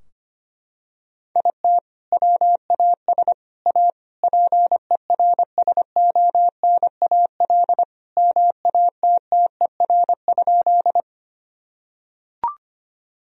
Morse code, heard as a keyed tone, about 25 wpm, 700 Hz. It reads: IT WAS A PERSONAL MATTER?
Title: Sentences from Top 1000 Words - Review Part 4 - 25wpm